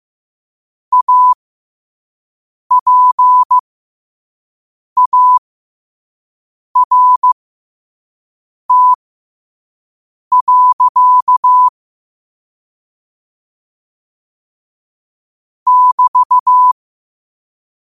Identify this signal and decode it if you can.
Morse code, heard as a keyed tone, about 15 wpm, spaced as for 5 wpm, 1000 Hz. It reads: APART. <BT>
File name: qotdfiles / 5wpm